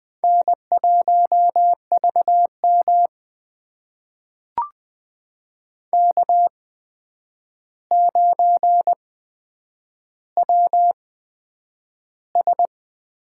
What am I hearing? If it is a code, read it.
Morse code, heard as a keyed tone, about 20 wpm, spaced as for 9 wpm, 700 Hz. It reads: N1VM K9WS
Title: USA Call Signs - 4 Characters - 8x Character Spacing - 20wpm